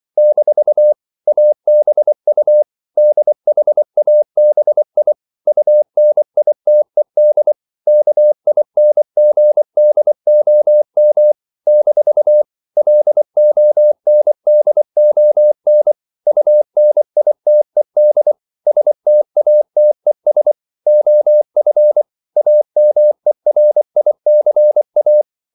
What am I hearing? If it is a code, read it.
Morse code, heard as a keyed tone, about 24 wpm, 600 Hz. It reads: - ABU DHABI UNITED KINGDOM - LONDON UNITED STATES OF AMERICA
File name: CW / Practice / NationalCapitals24